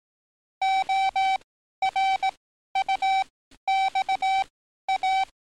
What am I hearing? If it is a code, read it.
Morse code, heard as a keyed tone, about 18 wpm, spaced as for 12 wpm, 770 Hz. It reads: ORUXA
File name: training / NAVPERS 11049to11060 / NAVPERS 11052 RA RB